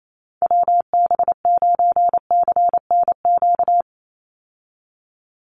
Morse code, heard as a keyed tone, about 28 wpm, 700 Hz. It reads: WB9CNQ
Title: USA Call Signs 28wpm